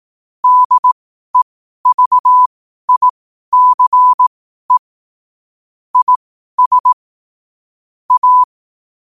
Morse code, heard as a keyed tone, about 18 wpm, spaced as for 12 wpm, 1000 Hz. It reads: DEVICE IS A
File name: qotdfiles / 12wpm